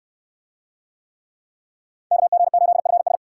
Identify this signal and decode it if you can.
Morse code, heard as a keyed tone, about 69 wpm, 700 Hz. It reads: KG8FS